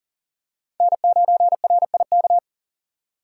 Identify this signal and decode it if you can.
Morse code, heard as a keyed tone, about 40 wpm, 700 Hz. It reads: N9RIK
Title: USA Call Signs - 5 Characters - 6x Character Spacing - 40wpm